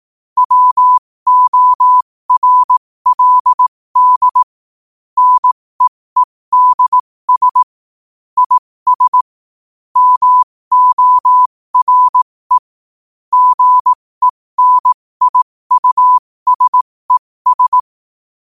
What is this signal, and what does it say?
Morse code, heard as a keyed tone, about 18 wpm, spaced as for 15 wpm, 1000 Hz. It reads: WORLD NEEDS IS MORE GENIUSES
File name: qotdfiles / 15wpm